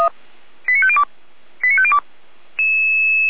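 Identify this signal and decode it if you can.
Five-tone selective call (ZVEI), heard as quick runs of five tones, tones of about 70 ms. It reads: 89502 89502
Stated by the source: ZVEI five-tone call received on SDR, audio archive of feuerwehr-orlamuende.org.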